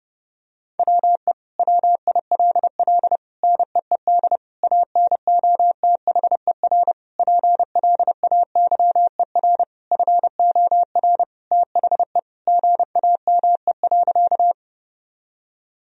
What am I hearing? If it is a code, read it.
Morse code, heard as a keyed tone, about 28 wpm, 700 Hz. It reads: WE WILL NEED ANOTHER PLAYER FOR THE GAME.
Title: Sentences from Top 1000 Words - Review Part 4 - 30wpm